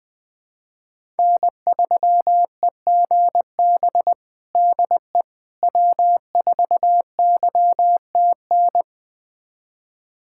Morse code, heard as a keyed tone, about 20 wpm, 700 Hz. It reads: N3EGB DE W4YTN
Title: USA Callsign DE Callsign - 5 Characters - 20wpm